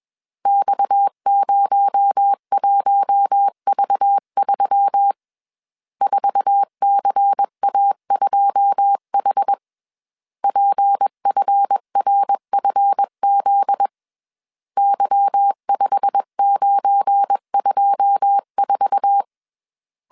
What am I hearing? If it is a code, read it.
Morse code, heard as a keyed tone, about 21 wpm, 800 Hz. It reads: X01V3 4CA2H PFRFZ Y5924
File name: CW / DET / 1 9 1500C S C 20wpm 2310